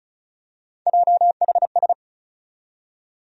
Morse code, heard as a keyed tone, about 34 wpm, 700 Hz. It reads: JHS
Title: USA Call Sign Postfixes - 3 Letters 35wpm